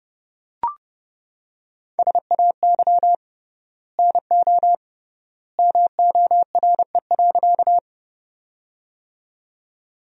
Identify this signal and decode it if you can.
Morse code, heard as a keyed tone, about 31 wpm, 700 Hz. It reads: SAY NO MORE.
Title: 3x Word-Spacing - Sentences from Top 200 Words - 30wpm